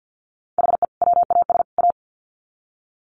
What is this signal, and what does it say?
Morse code, heard as a keyed tone, about 48 wpm, 700 Hz. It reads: HE WAS A
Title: Sets of 3 Words 50wpm